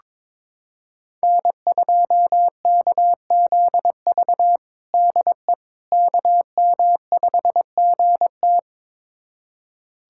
Morse code, heard as a keyed tone, about 21 wpm, 700 Hz. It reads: N2KZV DE KM5GT
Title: USA Callsign DE Callsign - 5 Characters - 22wpm